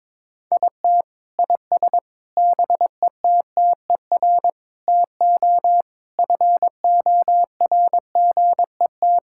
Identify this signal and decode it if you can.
Morse code, heard as a keyed tone, about 22 wpm, 700 Hz. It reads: IT IS BETTER TO FORGET
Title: Sentences from Top 1600 Words 22wpm